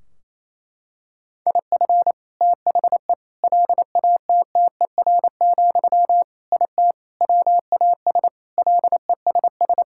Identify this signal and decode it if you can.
Morse code, heard as a keyed tone, about 28 wpm, 700 Hz. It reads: IF THE LATTER, IT WAS LESS
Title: Sentences from Top 1600 Words 28wpm